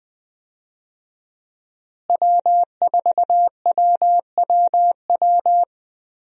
Morse code, heard as a keyed tone, about 20 wpm, 700 Hz. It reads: W4WWW
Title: USA Call Signs - 5 Characters - Repeated 2x - 20wpm